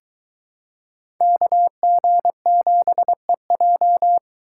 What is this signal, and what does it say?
Morse code, heard as a keyed tone, about 22 wpm, 700 Hz. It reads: KG7EJ